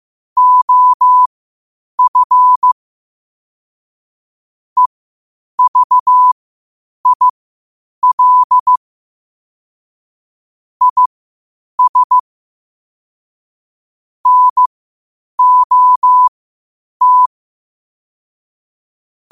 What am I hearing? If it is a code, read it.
Morse code, heard as a keyed tone, about 15 wpm, spaced as for 8 wpm, 1000 Hz. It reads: OF EVIL IS NOT
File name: qotdfiles / 8wpm